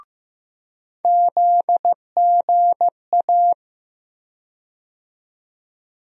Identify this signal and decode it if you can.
Morse code, heard as a keyed tone, about 15 wpm, 700 Hz. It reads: ZGA
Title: USA Call Sign Postfixes - 3 Letters 15wpm